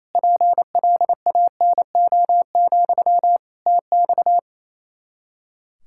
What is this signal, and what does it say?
Morse code, heard as a keyed tone, about 28 wpm, 700 Hz. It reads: PLANO, TX
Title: Practice QSO - Exchange 1 - Beginning - 28wpm